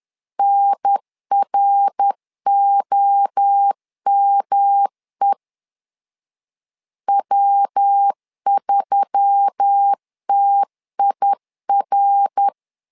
Morse code, heard as a keyed tone, about 10 wpm, 800 Hz. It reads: NROME W3TIR